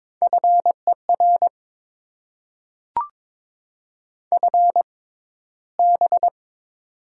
Morse code, heard as a keyed tone, about 22 wpm, spaced as for 13 wpm, 700 Hz. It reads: FER FB
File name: Top 20 QSO Elements - 6x Character Spacing - 22wpm